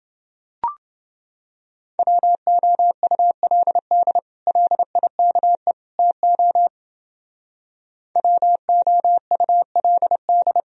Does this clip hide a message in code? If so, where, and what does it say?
Yes: Morse code from 2.0 to 10.6 s, heard as a keyed tone, about 29 wpm, 700 Hz. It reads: WOULD LIKE TO WOULD